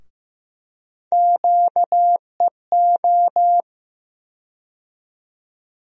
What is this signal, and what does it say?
Morse code, heard as a keyed tone, about 15 wpm, 700 Hz. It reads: QEO
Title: USA Call Sign Postfixes - 3 Letters 15wpm